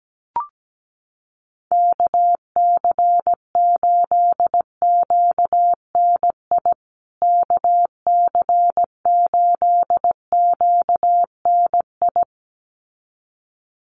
Morse code, heard as a keyed tone, about 17 wpm, 700 Hz. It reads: KC8QNI KC8QNI
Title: USA Call Signs Repeated 2x 17wpm